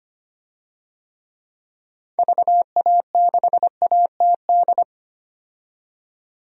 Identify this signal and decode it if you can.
Morse code, heard as a keyed tone, about 25 wpm, 700 Hz. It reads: VA6ATD
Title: DXCC-100 Call Signs - 6 Characters - 25wpm